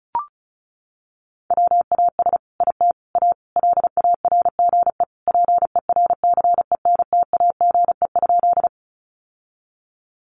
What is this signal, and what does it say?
Morse code, heard as a keyed tone, about 34 wpm, 700 Hz. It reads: WAS IT A LARGE PERCENTAGE?
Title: Sentences from Top 1300 Words 35wpm